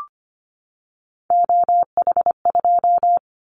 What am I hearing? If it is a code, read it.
Morse code, heard as a keyed tone, about 25 wpm, 700 Hz. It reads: OH2
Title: Callsign Prefixes for Top 25 DXCC - 25wpm